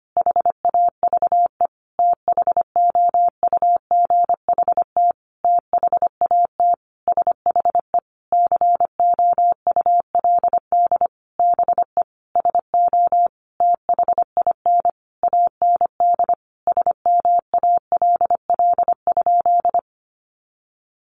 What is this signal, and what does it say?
Morse code, heard as a keyed tone, about 25 wpm, 700 Hz. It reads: HAVE THOUGHT THAT SHE COULD BE SO THIN AND SMALL?
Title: Sentences from Top 1200 Words 25wpm